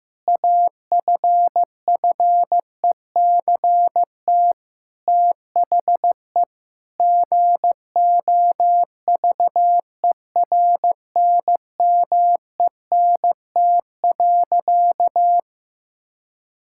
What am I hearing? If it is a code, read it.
Morse code, heard as a keyed tone, about 15 wpm, 700 Hz. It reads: AFFECT THE GOVERNMENT.
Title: Sentences from Top 1000 Words - Review Part 6 - 15wpm